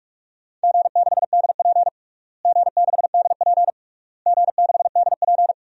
Morse code, heard as a keyed tone, about 46 wpm, 700 Hz. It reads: G6DP G6DP G6DP
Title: DXCC-100 Call Signs - 4 Characters - Repeated 3x - 45wpm